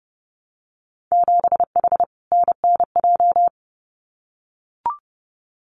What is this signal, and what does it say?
Morse code, heard as a keyed tone, about 30 wpm, 700 Hz. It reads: 7H NNJ
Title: Winter Field Day Exchanges Encore 30wpm